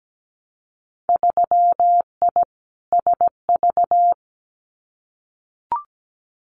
Morse code, heard as a keyed tone, about 17 wpm, 700 Hz. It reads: 3I SV
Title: Winter Field Day Exchanges Encore 17wpm